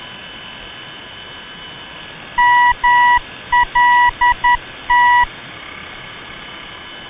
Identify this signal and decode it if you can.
Morse code, heard as a keyed tone, about 10 wpm, 970 Hz. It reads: MLT
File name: ndb malta mlt-395